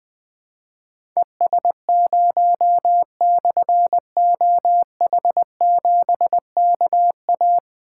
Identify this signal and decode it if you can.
Morse code, heard as a keyed tone, about 20 wpm, 700 Hz. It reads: ES0/OH7KA
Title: Difficult Call Signs 20wpm